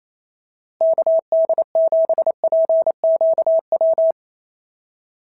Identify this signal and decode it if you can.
Morse code, heard as a keyed tone, about 28 wpm, 650 Hz. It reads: KD7PQW
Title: SOTA QSO Practice 28wpm